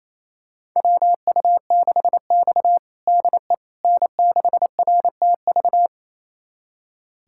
Morse code, heard as a keyed tone, about 28 wpm, 700 Hz. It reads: WU6X DE N6RTV
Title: USA Callsign DE Callsign 28wpm